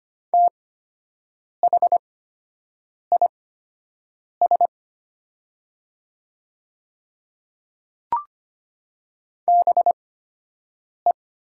Morse code, heard as a keyed tone, about 25 wpm, spaced as for 7 wpm, 700 Hz. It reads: THIS BE
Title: 8x Character Spacing - Top 500 Words - 25wpm